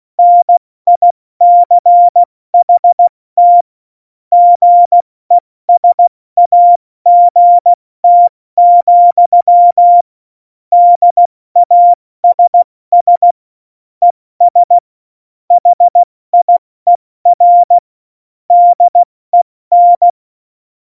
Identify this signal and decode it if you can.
Morse code, heard as a keyed tone, about 16 wpm, 700 Hz. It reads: NICHT GESAGT, DASS ES HIER DEN
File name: Morsekurs / Witze / Witz 17. November 2020, 70 F80